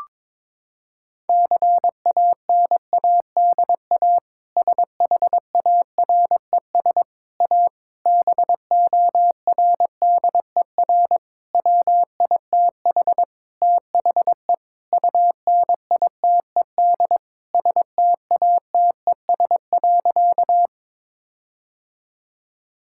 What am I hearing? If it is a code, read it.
Morse code, heard as a keyed tone, about 21 wpm, 700 Hz. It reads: CANADA SHARES A BORDER WITH THE UNITED STATES.
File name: practice / 22wpm / Sentences with 4-Word Phrases - 22wpm